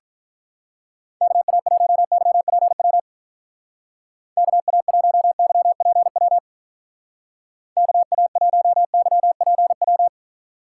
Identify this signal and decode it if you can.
Morse code, heard as a keyed tone, about 45 wpm, 700 Hz. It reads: KA1YPW KA1YPW KA1YPW